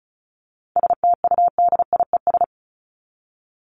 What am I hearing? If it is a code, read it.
Morse code, heard as a keyed tone, about 34 wpm, 700 Hz. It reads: STUDIES
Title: ES:S Suffix - Top 100 Words with Base Words 35wpm